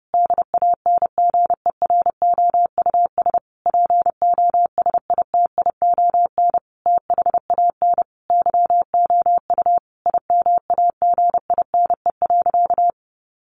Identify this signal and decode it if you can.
Morse code, heard as a keyed tone, about 30 wpm, 700 Hz. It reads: DANGEROUS POSITION THAN YOU IMAGINE.
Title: Sentences from Top 1400 Words 30wpm